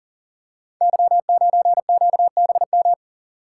Morse code, heard as a keyed tone, about 40 wpm, 690 Hz. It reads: Y9QBM
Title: Code Groups - 5 Letter:Numbers - 40wpm